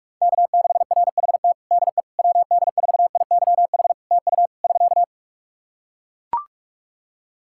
Morse code, heard as a keyed tone, about 44 wpm, 700 Hz. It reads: K6RHT DE WD4IYH TU <SK>
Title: Practice QSO - Exchange 4 - Ending - 45wpm